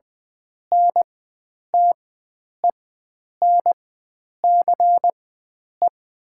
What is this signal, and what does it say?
Morse code, heard as a keyed tone, about 20 wpm, spaced as for 9 wpm, 700 Hz. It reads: NTENCE
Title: 4x Character Spacing - Top 300 Words - 20wpm